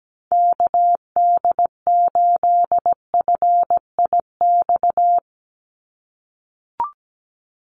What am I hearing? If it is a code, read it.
Morse code, heard as a keyed tone, about 17 wpm, 700 Hz. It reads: KD8FIX
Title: USA Call Signs 17wpm